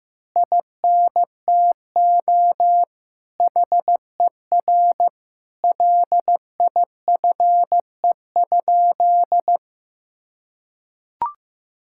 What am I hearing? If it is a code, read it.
Morse code, heard as a keyed tone, about 15 wpm, 700 Hz. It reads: INTO HER LIFE?
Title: Sentences from Top 1000 Words - Review Part 6 - 15wpm